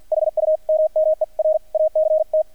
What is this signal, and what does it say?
Morse code, heard as a keyed tone, about 48 wpm, spaced as for 42 wpm, 620 Hz. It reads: HUMMER ACT
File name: archived / CRSnet